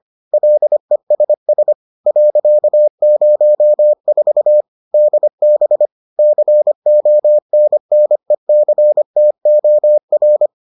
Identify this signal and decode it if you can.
Morse code, heard as a keyed tone, about 25 wpm, 600 Hz. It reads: LESS .04 DB CONNECTOR